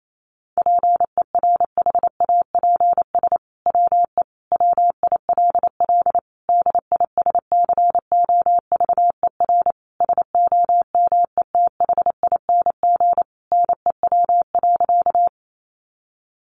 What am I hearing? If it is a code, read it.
Morse code, heard as a keyed tone, about 28 wpm, 700 Hz. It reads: PERHAPS WE WILL DISCOVER SOMETHING NEW.
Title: Sentences from Top 1100 Words 28wpm